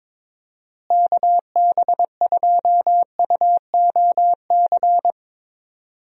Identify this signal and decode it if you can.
Morse code, heard as a keyed tone, about 22 wpm, 700 Hz. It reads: KB2UOC
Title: USA Call Signs - 6 Characters - Repeated 2x - 22wpm